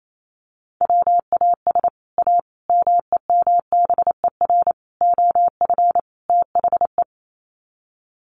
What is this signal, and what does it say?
Morse code, heard as a keyed tone, about 28 wpm, 700 Hz. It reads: WAS A MEMBER OF THE
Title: Sets of 5 Words 28wpm